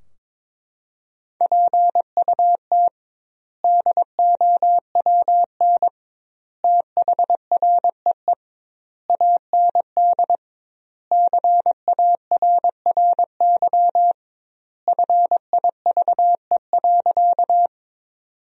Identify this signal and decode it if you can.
Morse code, heard as a keyed tone, about 22 wpm, 700 Hz. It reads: PUT DOWN THREE AND CARRY FIVE.
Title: 2x Word-Spacing - Sentences from Top 400 Words - 22wpm